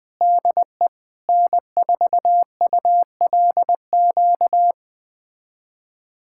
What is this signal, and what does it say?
Morse code, heard as a keyed tone, about 20 wpm, 700 Hz. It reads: DE N4ULQ
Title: USA Callsign DE Callsign - 5 Characters - 20wpm